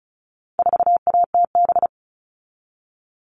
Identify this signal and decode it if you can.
Morse code, heard as a keyed tone, about 34 wpm, 700 Hz. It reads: 4ATB
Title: All Cut Numbers - 4-Digit 35wpm